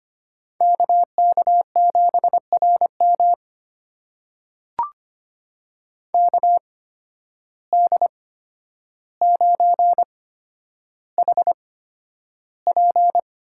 Morse code, heard as a keyed tone, about 25 wpm, spaced as for 11 wpm, 700 Hz. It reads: KK7RM KD9HP
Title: USA Call Signs - 8x Character Spacing - 25wpm